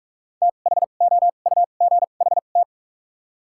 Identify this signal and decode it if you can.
Morse code, heard as a keyed tone, about 46 wpm, spaced as for 33 wpm, 700 Hz. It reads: THOUGHT
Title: 2x Character Spacing - Top 300 Words - 45wpm